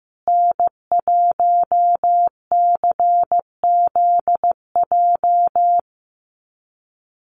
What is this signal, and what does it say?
Morse code, heard as a keyed tone, about 15 wpm, 700 Hz. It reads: N1CZJ
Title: USA Call Signs Repeated 3x 15wpm